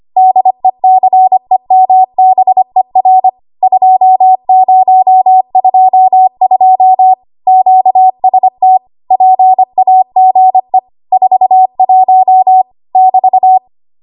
Morse code, heard as a keyed tone, about 25 wpm, 750 Hz. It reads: DECEMBER 2022 QST PAGE 41 <BT>